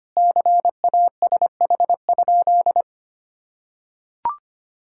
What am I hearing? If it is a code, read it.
Morse code, heard as a keyed tone, about 25 wpm, 700 Hz. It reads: CASH?
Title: Sentences from Top 1000 Words - Review Part 5 - 25wpm